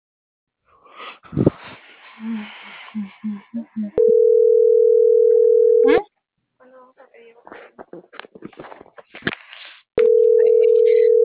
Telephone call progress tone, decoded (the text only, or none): ringback tone